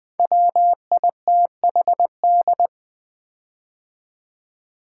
Morse code, heard as a keyed tone, about 20 wpm, 700 Hz. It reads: WITHD
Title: Build A Word - 5 Letters At A Time - 10-Letter Words - 20wpm